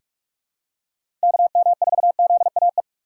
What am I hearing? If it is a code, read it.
Morse code, heard as a keyed tone, about 41 wpm, 700 Hz. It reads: KM4ZAE
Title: USA Call Signs - 3x Character Spacing - 45wpm